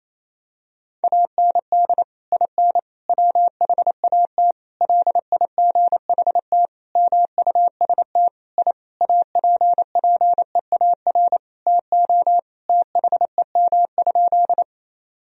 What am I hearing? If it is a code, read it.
Morse code, heard as a keyed tone, about 28 wpm, 700 Hz. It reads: AND IN WHAT LIGHT MUST I APPEAR TO THEM?
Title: Sentences from Top 1000 Words - Review Part 2 - 28wpm